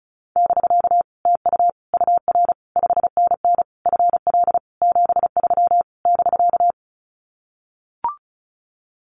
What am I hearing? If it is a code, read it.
Morse code, heard as a keyed tone, about 34 wpm, 700 Hz. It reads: <BK> TU UR 5NN FL 73 <BK>
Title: Common QSO Phrases Encore 35wpm